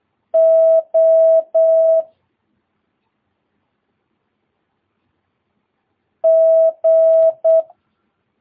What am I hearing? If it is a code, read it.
Morse code, heard as a keyed tone, about 8 wpm, 650 Hz. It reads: O G
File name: jh8cbh / 2022cw